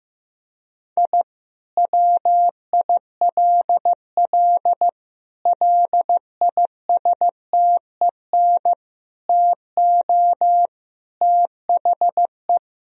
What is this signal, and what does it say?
Morse code, heard as a keyed tone, about 14 wpm, 700 Hz. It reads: I WILL LISTEN TO THE
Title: Sentences from Top 1000 Words - Review Part 3 - 15wpm